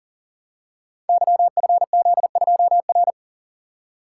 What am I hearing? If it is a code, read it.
Morse code, heard as a keyed tone, about 40 wpm, 700 Hz. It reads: YFZ2R
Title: Code Groups - 5 Letter:Numbers - 40wpm